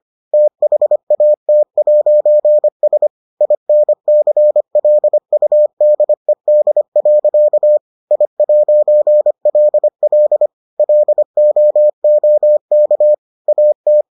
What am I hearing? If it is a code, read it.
Morse code, heard as a keyed tone, about 25 wpm, 600 Hz. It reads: THAT'S INCLUDED. I'LL LOOK AT